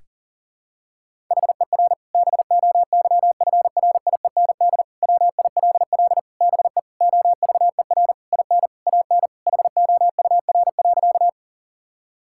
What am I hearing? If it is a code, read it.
Morse code, heard as a keyed tone, about 40 wpm, 700 Hz. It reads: HER BOYFRIEND WILL BE OVER IN AN HOUR.
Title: Sentences from Top 1000 Words - Review Part 5 - 40wpm